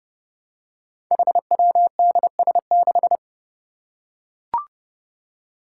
Morse code, heard as a keyed tone, about 30 wpm, 700 Hz. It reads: HWDS6
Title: Five Challenging Letter-Number Combinations 30wpm